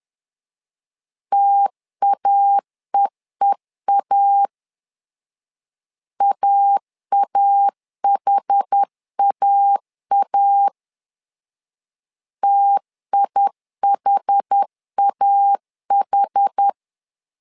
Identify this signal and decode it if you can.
Morse code, heard as a keyed tone, about 10 wpm, 800 Hz. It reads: TAEEA AAHAA TIHAH
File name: LEZIONI / 2 UHDN / 2 UH 10wpm 1714